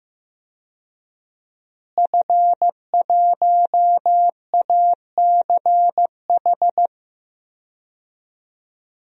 Morse code, heard as a keyed tone, about 15 wpm, 700 Hz. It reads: F1ACH
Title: DXCC-100 Call Signs - 5 Characters - Repeated 2x - 15wpm